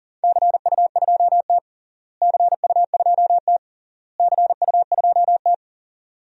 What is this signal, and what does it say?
Morse code, heard as a keyed tone, about 40 wpm, 700 Hz. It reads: CU2T CU2T CU2T
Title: DXCC-100 Call Signs - 4 Characters - Repeated 3x - 40wpm